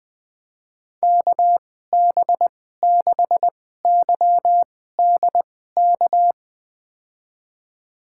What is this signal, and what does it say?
Morse code, heard as a keyed tone, about 20 wpm, spaced as for 14 wpm, 700 Hz. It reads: KB6YDK
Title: USA Call Signs - 6 Characters - 2x Character Spacing - 20wpm